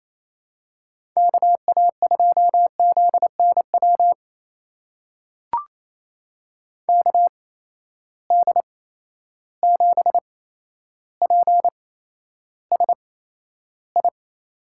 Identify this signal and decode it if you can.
Morse code, heard as a keyed tone, about 27 wpm, spaced as for 12 wpm, 700 Hz. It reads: KA2ZNW KD7PSI
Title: USA Call Signs - 6 Characters - 8x Character Spacing - 28wpm